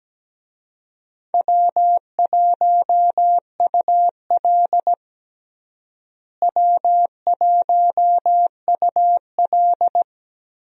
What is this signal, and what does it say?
Morse code, heard as a keyed tone, about 17 wpm, 700 Hz. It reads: W1UL W1UL
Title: USA Call Signs - 4 Characters - Repeated 2x - 17wpm